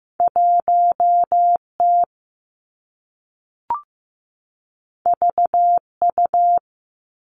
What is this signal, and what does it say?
Morse code, heard as a keyed tone, about 15 wpm, 700 Hz. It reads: 1 T V U
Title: Two Challenging Letter-Number Combinations 15wpm